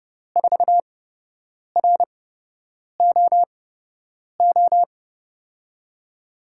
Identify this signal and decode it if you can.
Morse code, heard as a keyed tone, about 30 wpm, spaced as for 8 wpm, 700 Hz. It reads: 4ROO
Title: USA Call Signs - 6 Characters - 8x Character Spacing - 30wpm